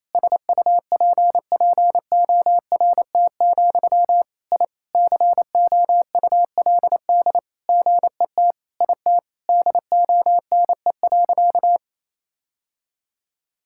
Morse code, heard as a keyed tone, about 28 wpm, 700 Hz. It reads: SUPPORT, I COULD GET IT DONE.
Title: Sentences from Top 1000 Words - Review Part 5 - 28wpm